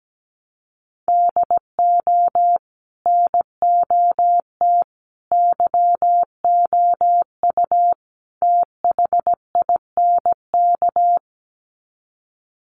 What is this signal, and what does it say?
Morse code, heard as a keyed tone, about 17 wpm, 700 Hz. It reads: DO NOT YOU THINK
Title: Sets of 4 Words 17wpm